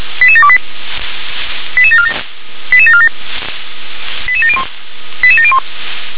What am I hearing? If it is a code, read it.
Five-tone selective call (ZVEI), heard as quick runs of five tones, tones of about 70 ms. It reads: 88728 88747 88747 88701 88701
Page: SDR/ZVEI